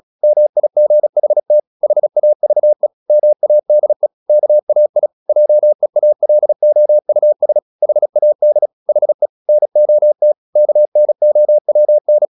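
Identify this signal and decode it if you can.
Morse code, heard as a keyed tone, about 36 wpm, 600 Hz. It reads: MIGHT HAVE MADE KAI JEALOUS HAD HE NOT KNOWN
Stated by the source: Morse practice at 36 wpm, text of Ken Follett's novel Never, from CW med Gnister.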